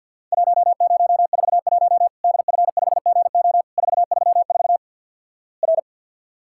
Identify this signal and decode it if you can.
Morse code, heard as a keyed tone, about 50 wpm, 700 Hz. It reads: 1041 DF5GO 434 R
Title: Worked All Europe DX Contest - QTC Exchange with Repeats - 50wpm